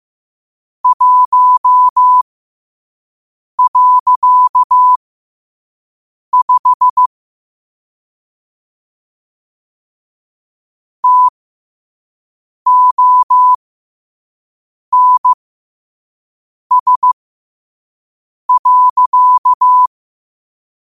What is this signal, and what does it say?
Morse code, heard as a keyed tone, about 15 wpm, spaced as for 5 wpm, 1000 Hz. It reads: 1.5 TONS.
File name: qotdfiles / 5wpm